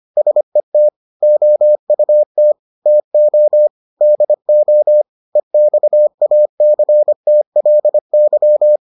Morse code, heard as a keyed tone, about 25 wpm, 600 Hz. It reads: SET OUT TO DO EXACTLY